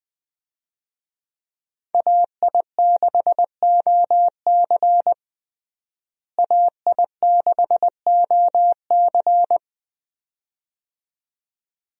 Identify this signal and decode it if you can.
Morse code, heard as a keyed tone, about 20 wpm, 700 Hz. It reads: AI6OC AI6OC
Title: USA Call Signs - 5 Characters - Repeated 2x - 20wpm